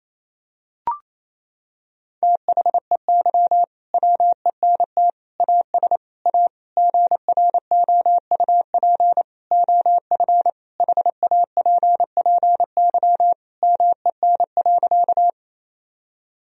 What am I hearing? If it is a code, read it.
Morse code, heard as a keyed tone, about 28 wpm, 700 Hz. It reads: THEY WENT AS A GROUP OF HAPPY MEN.
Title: Sentences from Top 1000 Words - Review Part 1 - 28wpm